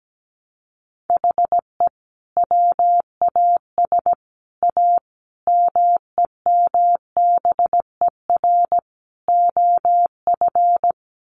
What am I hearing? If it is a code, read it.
Morse code, heard as a keyed tone, about 17 wpm, 700 Hz. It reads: HE WAS A MEMBER OF
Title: Sets of 5 Words Encore2 17wpm